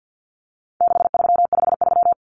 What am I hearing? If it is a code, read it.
Morse code, heard as a keyed tone, about 50 wpm, 700 Hz. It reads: 6353
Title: Numbers - 4-Digit 50wpm